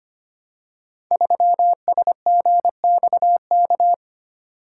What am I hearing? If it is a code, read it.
Morse code, heard as a keyed tone, about 25 wpm, 700 Hz. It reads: 3SGXK
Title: Code Groups - 5 Letter:Numbers - 25wpm